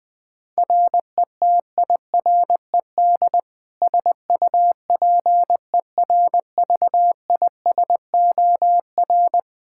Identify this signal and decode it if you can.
Morse code, heard as a keyed tone, about 21 wpm, 700 Hz. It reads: RETIRED SUPERVISOR
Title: Practice QSO - Exchange 3 - More Info - 20wpm